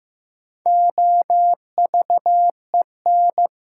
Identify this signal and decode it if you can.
Morse code, heard as a keyed tone, about 15 wpm, 700 Hz. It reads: OVEN
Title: Sentences from Top 1200 Words - New Word Drill Alt - 15wpm